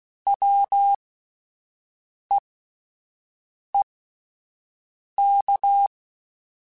Morse code, heard as a keyed tone, about 16 wpm, spaced as for 5 wpm, 800 Hz. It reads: WEEK
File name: Alphabetical / Alphabetical mp3s / Ws @5